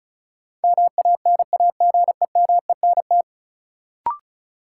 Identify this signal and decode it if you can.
Morse code, heard as a keyed tone, about 36 wpm, 700 Hz. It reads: MANAGEMENT
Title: Sentences from Top 1000 Words - Review Part 4 - New Word Drill - 35wpm